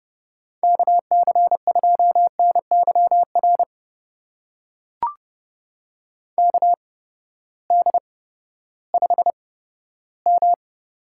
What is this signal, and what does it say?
Morse code, heard as a keyed tone, about 30 wpm, spaced as for 15 wpm, 700 Hz. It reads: KC2NYR KD5M
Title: USA Call Signs - 6 Characters - 8x Character Spacing - 30wpm